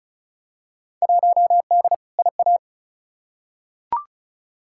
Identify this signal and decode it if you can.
Morse code, heard as a keyed tone, about 34 wpm, 700 Hz. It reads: 1D IA
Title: Field Day Exchanges - 35wpm